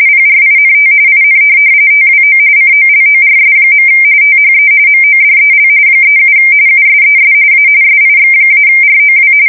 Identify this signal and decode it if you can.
RTTY, about 45.45 baud. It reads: PRINT IN TWO COLORS. THE MODEL 37 USES A SIX-ROW
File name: fsk170 / teletype-corp